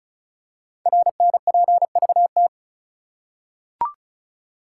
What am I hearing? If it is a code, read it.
Morse code, heard as a keyed tone, about 36 wpm, 700 Hz. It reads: RNPVT
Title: Code Groups - 5 Letters - 35wpm